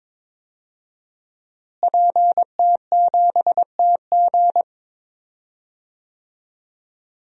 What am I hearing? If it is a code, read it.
Morse code, heard as a keyed tone, about 22 wpm, 700 Hz. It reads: PT7TG
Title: DXCC-100 Call Signs - 5 Characters - Repeated 2x - 22wpm